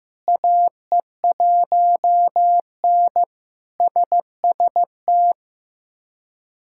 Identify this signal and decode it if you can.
Morse code, heard as a keyed tone, about 15 wpm, 700 Hz. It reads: AE1N SST
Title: K1USN SST QSO Practice - 15wpm